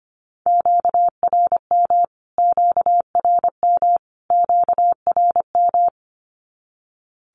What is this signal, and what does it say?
Morse code, heard as a keyed tone, about 25 wpm, 700 Hz. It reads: QRM QRM QRM
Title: Q-Codes Repeated 3x 25wpm